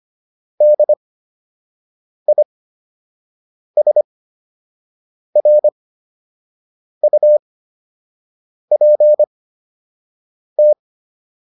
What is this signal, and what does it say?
Morse code, heard as a keyed tone, about 25 wpm, spaced as for 6 wpm, 600 Hz. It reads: DISRUPT